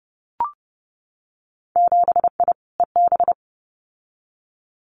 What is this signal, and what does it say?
Morse code, heard as a keyed tone, about 30 wpm, 700 Hz. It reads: 7I EB
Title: Winter Field Day Exchanges Encore 30wpm